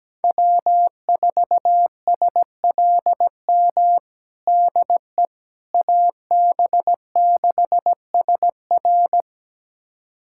Morse code, heard as a keyed tone, about 17 wpm, 700 Hz. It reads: W4SLM DE AB6SR
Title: USA Callsign DE Callsign - 5 Characters - 17wpm